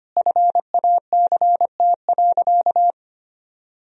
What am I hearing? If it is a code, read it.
Morse code, heard as a keyed tone, about 25 wpm, 700 Hz. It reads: FACT.
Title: Sentences from Top 1000 Words - Review Part 3 - 25wpm